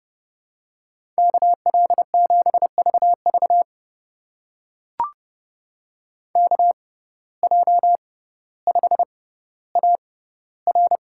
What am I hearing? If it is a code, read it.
Morse code, heard as a keyed tone, about 28 wpm, spaced as for 15 wpm, 700 Hz. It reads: KL7VV KJ5AR
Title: USA Call Signs - 5 Characters - 6x Character Spacing - 30wpm